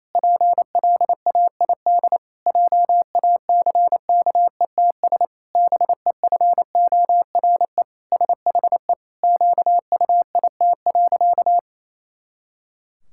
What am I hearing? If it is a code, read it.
Morse code, heard as a keyed tone, about 28 wpm, 700 Hz. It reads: PLAID JACKETS BEFORE SHE QUIT.
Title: Pangram Challenge v2 - 28wpm